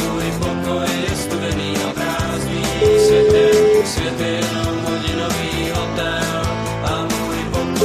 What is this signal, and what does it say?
Ringback tone (Europe, 425 Hz), with noise.